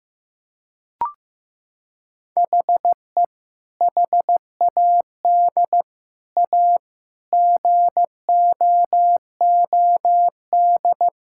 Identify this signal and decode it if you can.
Morse code, heard as a keyed tone, about 15 wpm, 700 Hz. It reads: HE HAD A GOOD
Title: Sentences from Top 1000 Words - Review Part 6 - 15wpm